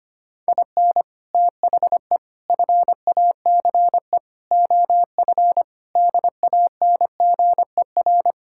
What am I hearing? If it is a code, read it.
Morse code, heard as a keyed tone, about 25 wpm, 700 Hz. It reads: IN THE FACE OF DANGER
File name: practice / 25wpm / Sentences with 4-Word Phrases - 25wpm